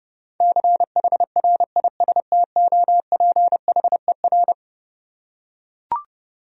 Morse code, heard as a keyed tone, about 32 wpm, 700 Hz. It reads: CHRISTOPHER
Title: USA Common Names - 30wpm